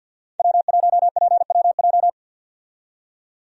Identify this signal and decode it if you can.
Morse code, heard as a keyed tone, about 48 wpm, 700 Hz. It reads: W1PWJ
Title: USA Call Signs - 5 Characters - 50wpm